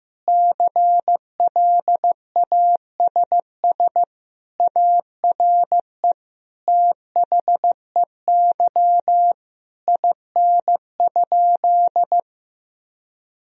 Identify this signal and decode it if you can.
Morse code, heard as a keyed tone, about 16 wpm, 700 Hz. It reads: CLASS ARE THEY IN?
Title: Sentences from Top 1000 Words - Review Part 2 - 15wpm